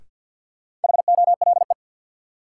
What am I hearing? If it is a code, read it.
Morse code, heard as a keyed tone, about 52 wpm, 700 Hz. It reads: HOLE
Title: Sentences from Top 1600 Words - New Word Drill Alt - 50wpm